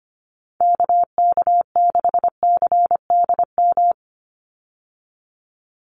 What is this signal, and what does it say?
Morse code, heard as a keyed tone, about 25 wpm, 700 Hz. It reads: KK6CDM
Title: USA Call Signs 25wpm